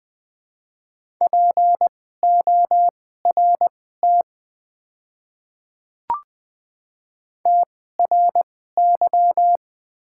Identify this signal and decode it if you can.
Morse code, heard as a keyed tone, about 20 wpm, spaced as for 14 wpm, 700 Hz. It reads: PORT TRY
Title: 2x Character Spacing - Top 300 Words - 20wpm